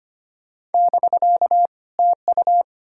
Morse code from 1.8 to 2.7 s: TU

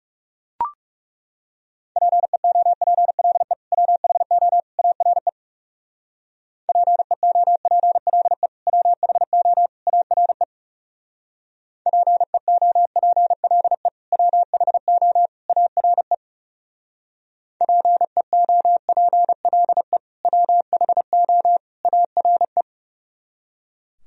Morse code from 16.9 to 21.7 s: PEOPLE WHO

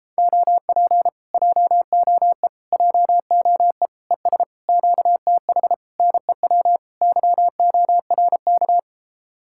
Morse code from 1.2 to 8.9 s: JOE JOE ES QTH NEW YORK

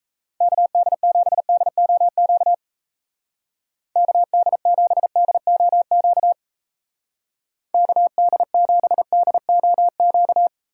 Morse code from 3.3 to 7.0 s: KD7DOQ

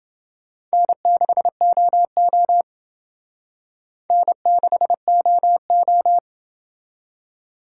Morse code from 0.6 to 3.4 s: N6OO

